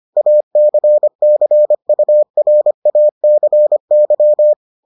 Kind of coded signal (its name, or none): Morse code